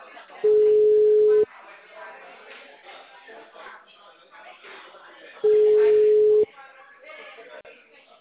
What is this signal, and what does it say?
Ringback tone (Europe, 425 Hz), with speech.